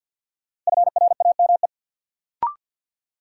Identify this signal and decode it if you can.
Morse code, heard as a keyed tone, about 47 wpm, 700 Hz. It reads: FRAME
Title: Sentences from Top 1000 Words - Review Part 6 - New Word Drill - 50wpm